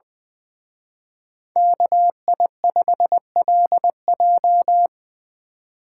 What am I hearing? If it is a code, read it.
Morse code, heard as a keyed tone, about 24 wpm, 700 Hz. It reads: KI5LJ
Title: Speed Racing - USA 2x2 Call Signs 20wpm